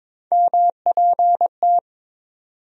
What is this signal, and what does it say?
Morse code, heard as a keyed tone, about 22 wpm, 700 Hz. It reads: MPT
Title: USA Call Sign Postfixes - 3 Letters 22wpm